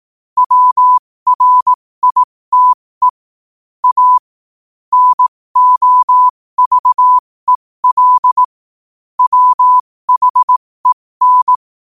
Morse code, heard as a keyed tone, about 18 wpm, spaced as for 15 wpm, 1000 Hz. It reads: WRITE A NOVEL WHEN
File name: qotdfiles / 15wpm